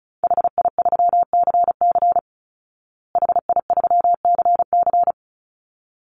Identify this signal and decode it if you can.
Morse code, heard as a keyed tone, about 34 wpm, 700 Hz. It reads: HI3CC HI3CC
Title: Call Signs Repeated 3x - 35wpm